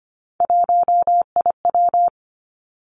Morse code, heard as a keyed tone, about 25 wpm, 700 Hz. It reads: 1IW
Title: Three Challenging Letter-Number Combinations 25wpm